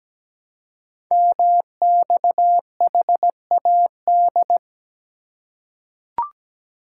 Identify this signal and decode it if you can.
Morse code, heard as a keyed tone, about 17 wpm, 700 Hz. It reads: MXHAD
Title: Code Groups - 5 Letters - 17wpm